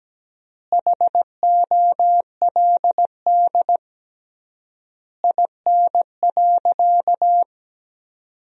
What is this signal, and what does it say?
Morse code, heard as a keyed tone, about 17 wpm, 700 Hz. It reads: HOLD IN.
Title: 3x Word-Spacing - Sentences from Top 400 Words - 17wpm